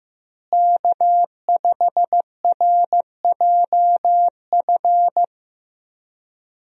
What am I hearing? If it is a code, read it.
Morse code, heard as a keyed tone, about 15 wpm, 700 Hz. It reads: K5RJF
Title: USA Call Signs - 5 Characters - Repeated 2x - 15wpm